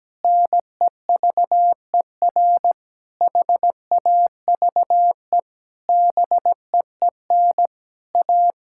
Morse code, heard as a keyed tone, about 17 wpm, 700 Hz. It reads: NEVER HAVE BEEN A
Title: Sentences from Top 1000 Words - Review Part 6 - 17wpm